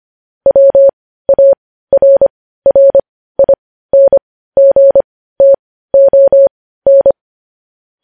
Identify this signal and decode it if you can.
Morse code, heard as a keyed tone, about 25 wpm, spaced as for 15 wpm, 550 Hz. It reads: WARRINGTON
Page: News headlines at 15